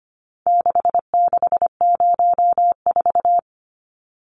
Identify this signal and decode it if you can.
Morse code, heard as a keyed tone, about 25 wpm, 700 Hz. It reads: 6604
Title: Numbers - 4-Digit 25wpm